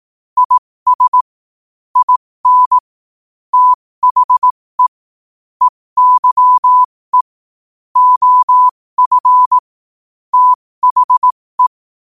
Morse code, heard as a keyed tone, about 18 wpm, spaced as for 15 wpm, 1000 Hz. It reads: IS IN THE EYE OF THE
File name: qotdfiles / 15wpm